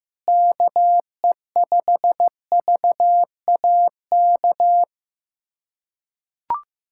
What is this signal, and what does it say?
Morse code, heard as a keyed tone, about 15 wpm, 700 Hz. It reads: KE5VAK
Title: USA Callsign DE Callsign 15wpm